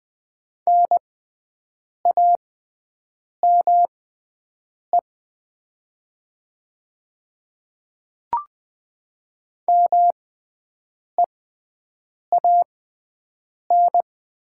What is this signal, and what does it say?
Morse code, heard as a keyed tone, about 20 wpm, spaced as for 7 wpm, 700 Hz. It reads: NAME MEAN
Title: 6x Character Spacing - Top 200 Words - 20wpm